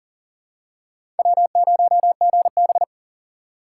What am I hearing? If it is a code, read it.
Morse code, heard as a keyed tone, about 38 wpm, 700 Hz. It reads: W0GB
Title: USA Call Signs - 4 Characters - 2x Character Spacing - 40wpm